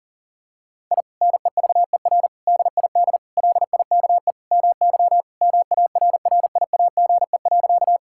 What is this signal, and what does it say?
Morse code, heard as a keyed tone, about 40 wpm, 700 Hz. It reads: I NEVER DID LIKE MY MARRIAGE.
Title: Sentences from Top 1000 Words - Review Part 5 - 40wpm